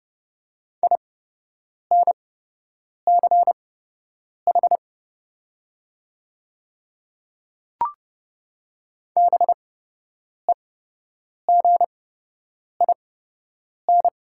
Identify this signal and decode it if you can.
Morse code, heard as a keyed tone, about 30 wpm, spaced as for 8 wpm, 700 Hz. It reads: INCH BEGIN
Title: 8x Character Spacing - Top 500 Words - 30wpm